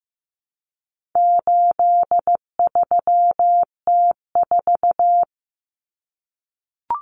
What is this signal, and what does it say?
Morse code, heard as a keyed tone, about 15 wpm, 700 Hz. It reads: 83T4
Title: Common Cut Numbers - 4-Digit 15wpm